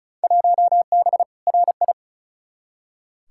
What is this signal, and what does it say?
Morse code, heard as a keyed tone, about 36 wpm, 700 Hz. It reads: 1B RI
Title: Field Day Exchanges Encore - 35wpm